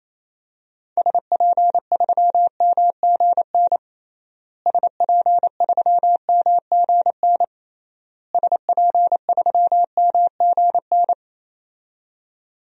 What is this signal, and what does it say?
Morse code, heard as a keyed tone, about 28 wpm, 700 Hz. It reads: SP3MGN SP3MGN SP3MGN
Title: DXCC-100 Call Signs - 6 Characters - Repeated 3x - 28wpm